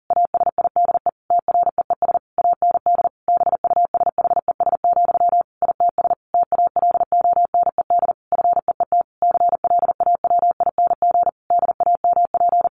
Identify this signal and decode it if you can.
Morse code, heard as a keyed tone, about 40 wpm, 700 Hz. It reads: ASIDE TREES AND BUSHES, ITS TALONED FEET CLAWING DAMP
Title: A Sound Of Thunder 40wpm